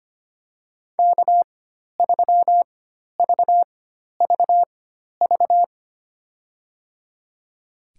Morse code, heard as a keyed tone, about 25 wpm, spaced as for 12 wpm, 700 Hz. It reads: K3VVV
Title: USA Call Signs - 5 Characters - 4x Character Spacing - 25wpm